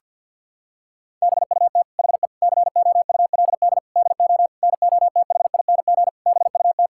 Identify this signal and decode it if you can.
Morse code, heard as a keyed tone, about 50 wpm, 700 Hz. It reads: BUT HE COULD DO NOTHING BUT
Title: Sentences from Top 1600 Words 50wpm